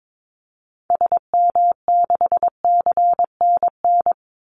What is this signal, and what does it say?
Morse code, heard as a keyed tone, about 22 wpm, 700 Hz. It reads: SM6CNN
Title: Call Signs Repeated 2x - 22wpm